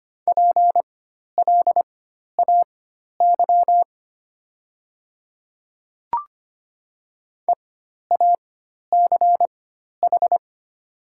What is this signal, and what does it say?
Morse code, heard as a keyed tone, about 25 wpm, spaced as for 12 wpm, 690 Hz. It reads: PLAY EACH
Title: 4x Character Spacing - Top 200 Words - 25wpm